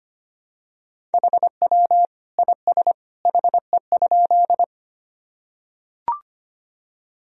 Morse code, heard as a keyed tone, about 25 wpm, 700 Hz. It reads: HW IS HE?
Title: Sentences from Top 100 Words and Abbreviations 25wpm